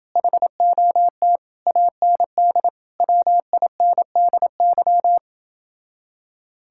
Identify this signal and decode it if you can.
Morse code, heard as a keyed tone, about 29 wpm, 700 Hz. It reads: HOT AND WINDY